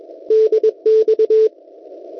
Morse code, heard as a keyed tone, about 21 wpm, 440 Hz. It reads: DX
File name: Test Pixid 160m / kiwi.oe9.at 1825.90 cw